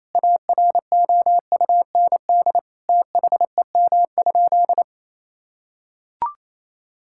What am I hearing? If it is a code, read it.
Morse code, heard as a keyed tone, about 28 wpm, 700 Hz. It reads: AROUND THEM?
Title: Sentences from Top 1000 Words - Review Part 5 - 28wpm